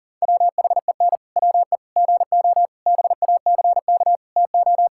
Morse code, heard as a keyed tone, about 40 wpm, 700 Hz. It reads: WHEN WE GO BACK TO
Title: Pangram Challenge v2 - 40wpm